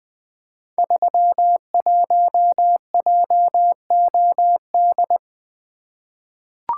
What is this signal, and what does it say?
Morse code, heard as a keyed tone, about 20 wpm, 700 Hz. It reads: 31JOD
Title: Code Groups - 5 Letter:Numbers - 20wpm